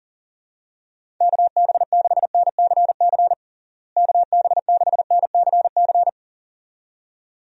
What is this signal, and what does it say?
Morse code, heard as a keyed tone, about 40 wpm, 700 Hz. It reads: KB6NCC KB6NCC
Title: USA Call Signs - 6 Characters - Repeated 2x - 40wpm